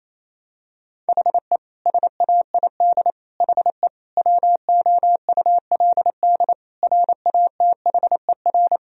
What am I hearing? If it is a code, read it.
Morse code, heard as a keyed tone, about 28 wpm, 700 Hz. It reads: HE SAID HE WOULD RATHER
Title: Sentences from Top 1000 Words - Review Part 4 - 28wpm